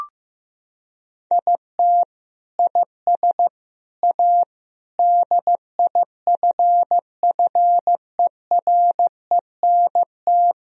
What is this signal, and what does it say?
Morse code, heard as a keyed tone, about 15 wpm, 700 Hz. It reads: IT IS A DIFFERENT
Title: Sentences from Top 1000 Words - Review Part 3 - 15wpm